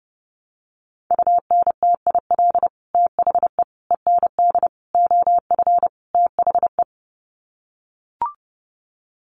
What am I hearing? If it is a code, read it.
Morse code, heard as a keyed tone, about 30 wpm, 700 Hz. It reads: UNTIL THE END OF THE
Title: Sets of 5 Words Encore 30wpm